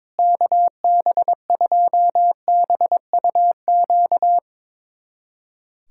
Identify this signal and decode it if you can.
Morse code, heard as a keyed tone, about 22 wpm, 700 Hz. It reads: KB2BUQ
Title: USA Callsign DE Callsign - 6 Characters - 22wpm